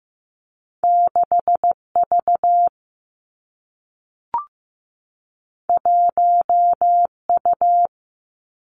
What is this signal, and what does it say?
Morse code, heard as a keyed tone, about 15 wpm, 700 Hz. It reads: 6 V 1 U
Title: Two Challenging Letter-Number Combinations 15wpm